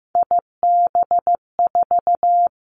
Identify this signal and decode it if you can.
Morse code, heard as a keyed tone, about 15 wpm, 700 Hz. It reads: IB4
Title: Three Challenging Letter-Number Combinations 15wpm